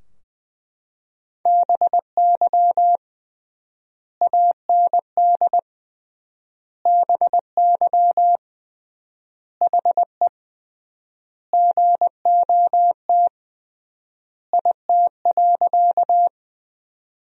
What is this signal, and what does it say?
Morse code, heard as a keyed tone, about 20 wpm, 700 Hz. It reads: BY AND BY HE GOT IT.
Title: 3x Word-Spacing - Sentences from Top 300 Words - 20wpm